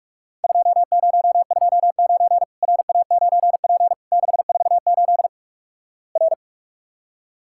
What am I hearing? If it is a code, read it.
Morse code, heard as a keyed tone, about 46 wpm, 700 Hz. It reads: 2029 RA9P 648 R
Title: Worked All Europe DX Contest - QTC Exchange - 45wpm